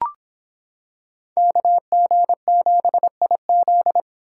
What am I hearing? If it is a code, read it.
Morse code, heard as a keyed tone, about 25 wpm, 700 Hz. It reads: KG7IZ